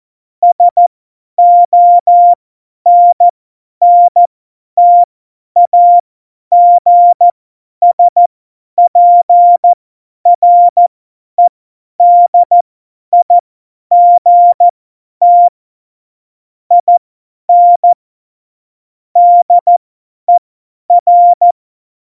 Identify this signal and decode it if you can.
Morse code, heard as a keyed tone, about 14 wpm, spaced as for 10 wpm, 700 Hz. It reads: SONNTAGSPREDIGT IN DER
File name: Morsekurs / Witze / Witz 01. Dezember 2020, 50 F70